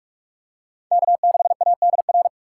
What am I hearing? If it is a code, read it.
Morse code, heard as a keyed tone, about 41 wpm, 700 Hz. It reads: K6ADR